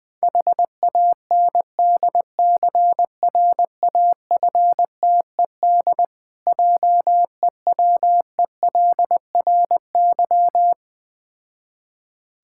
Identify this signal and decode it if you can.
Morse code, heard as a keyed tone, about 21 wpm, 690 Hz. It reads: HANDCRAFTED JEWELRY
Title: Difficult Words with Hints - 20wpm